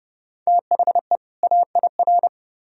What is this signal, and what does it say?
Morse code, heard as a keyed tone, about 29 wpm, 700 Hz. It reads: THE AIR